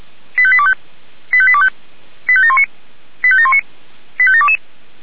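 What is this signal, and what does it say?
Five-tone selective call (ZVEI), heard as quick runs of five tones, tones of about 70 ms. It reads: 85825 85825 85719 85719 85722
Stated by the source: ZVEI five-tone selective call received on SDR.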